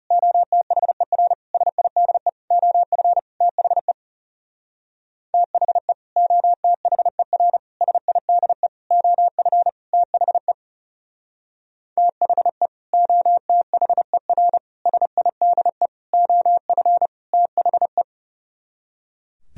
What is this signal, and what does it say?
Morse code, heard as a keyed tone, about 37 wpm, 700 Hz. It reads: OTHER SIDE OF THE THE OTHER SIDE OF THE THE OTHER SIDE OF THE
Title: Speed-Racing - Sets of 5 Words 30wpm